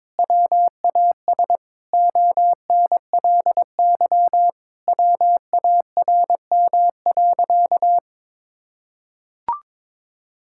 Morse code, heard as a keyed tone, about 22 wpm, 700 Hz. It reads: WAS ONLY WARM.
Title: Sentences from Top 1000 Words - Review Part 6 - 22wpm